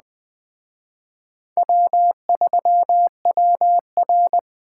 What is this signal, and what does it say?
Morse code, heard as a keyed tone, about 20 wpm, 700 Hz. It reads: W3WR
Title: Speed Racing - USA Call Signs 17wpm